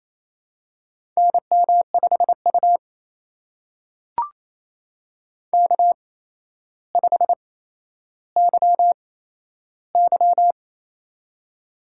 Morse code, heard as a keyed tone, about 27 wpm, spaced as for 12 wpm, 700 Hz. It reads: NM5U K5YY
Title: USA Call Signs - 4 Characters - 8x Character Spacing - 28wpm